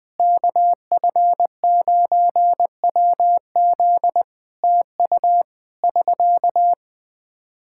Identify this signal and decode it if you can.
Morse code, heard as a keyed tone, about 21 wpm, 700 Hz. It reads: KF9WZ TU <SK>
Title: K1USN SST QSO Practice 20wpm